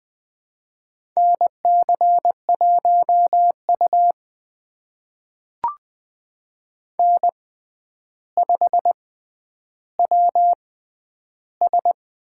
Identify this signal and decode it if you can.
Morse code, heard as a keyed tone, about 20 wpm, spaced as for 10 wpm, 700 Hz. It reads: NC1U N5WS
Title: USA Call Signs - 6x Character Spacing - 20wpm